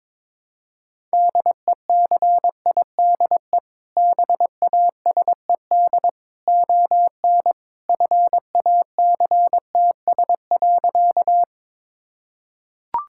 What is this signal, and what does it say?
Morse code, heard as a keyed tone, about 22 wpm, 700 Hz. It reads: DECIDE BASED ON FACTS.
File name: practice / 22wpm / Sentences with 2-Word Phrases - 22wpm